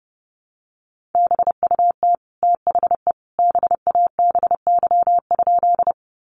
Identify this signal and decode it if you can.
Morse code, heard as a keyed tone, about 30 wpm, 700 Hz. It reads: BUT THE BABY?
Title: Sentences from Top 1300 Words 30wpm